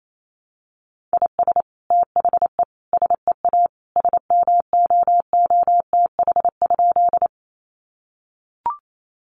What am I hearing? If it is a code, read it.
Morse code, heard as a keyed tone, about 28 wpm, 700 Hz. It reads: IS THE SEA SMOOTH?
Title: Sentences from Top 1400 Words 28wpm